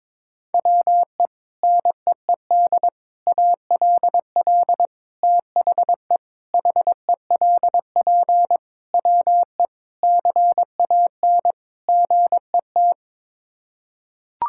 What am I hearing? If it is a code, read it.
Morse code, heard as a keyed tone, about 21 wpm, 700 Hz. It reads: WE NEED ALL THE HELP WE CAN GET
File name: practice / 22wpm / Sets of 8 Words - 22wpm